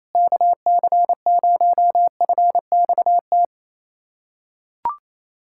Morse code, heard as a keyed tone, about 28 wpm, 700 Hz. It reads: KC0FXT
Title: USA Callsign DE Callsign - 6 Characters - 28wpm